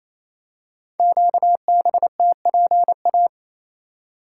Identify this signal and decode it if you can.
Morse code, heard as a keyed tone, about 29 wpm, 700 Hz. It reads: QBTPA
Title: Code Groups - 5 Letters - 28wpm